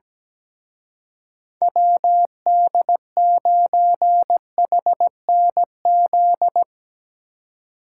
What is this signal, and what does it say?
Morse code, heard as a keyed tone, about 18 wpm, 700 Hz. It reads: WD9HNZ
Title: Speed Racing - USA 2x3 Call Signs 17wpm